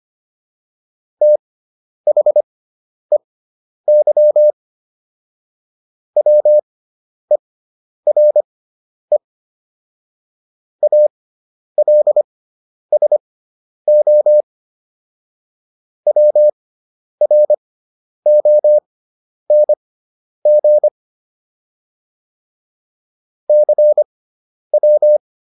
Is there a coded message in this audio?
Yes: Morse code from 1.2 to 25.2 s, heard as a keyed tone, about 25 wpm, spaced as for 10 wpm, 600 Hz. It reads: THEY WERE ALSO WRONG CW